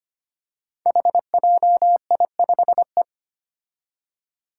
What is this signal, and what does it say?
Morse code, heard as a keyed tone, about 25 wpm, 700 Hz. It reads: HJI5E
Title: Five Challenging Letter-Number Combinations 25wpm